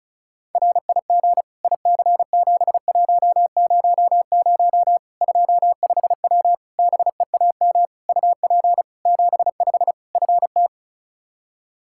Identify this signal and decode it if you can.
Morse code, heard as a keyed tone, about 34 wpm, 700 Hz. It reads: RIG IC7100 25W BEAM UP 75 FT
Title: Practice QSO - Exchange 2 - Info - 35wpm